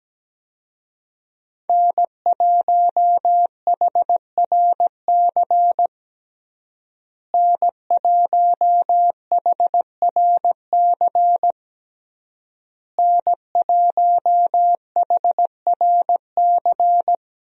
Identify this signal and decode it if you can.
Morse code, heard as a keyed tone, about 17 wpm, 700 Hz. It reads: N1HRC N1HRC N1HRC
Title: USA Call Signs - 5 Characters - Repeated 3x - 17wpm